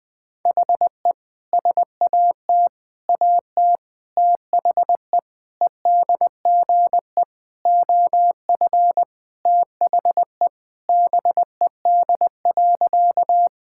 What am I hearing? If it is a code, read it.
Morse code, heard as a keyed tone, about 20 wpm, 700 Hz. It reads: HE SAT AT THE EDGE OF THE BED.
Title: Sentences with 5-Word Phrases - Spoken Beginning - 20wpm